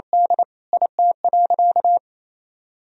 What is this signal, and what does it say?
Morse code, heard as a keyed tone, about 28 wpm, 700 Hz. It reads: D IT.
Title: Sentences from Top 1000 Words - Review Part 1 - 28wpm